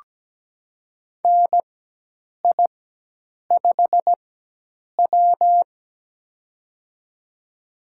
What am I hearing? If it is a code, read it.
Morse code, heard as a keyed tone, about 17 wpm, spaced as for 8 wpm, 700 Hz. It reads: NI5W
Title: USA Call Signs - 4 Characters - 4x Character Spacing - 17wpm